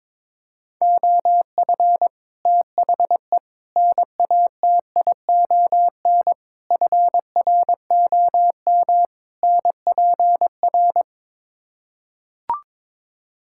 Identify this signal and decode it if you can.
Morse code, heard as a keyed tone, about 22 wpm, 700 Hz. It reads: OF THE NATION FROM NPR
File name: practice / 22wpm / Build A Phrase - 5-Word Phrases - 22wpm